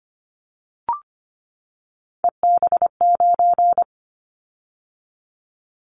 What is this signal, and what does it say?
Morse code, heard as a keyed tone, about 25 wpm, 700 Hz. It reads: EB9
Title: Callsign Prefixes for Top 100 DXCC - 25wpm